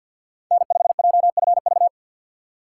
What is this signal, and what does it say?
Morse code, heard as a keyed tone, about 48 wpm, 700 Hz. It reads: N5JFV
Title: USA Call Signs - 5 Characters - 6x Character Spacing - 50wpm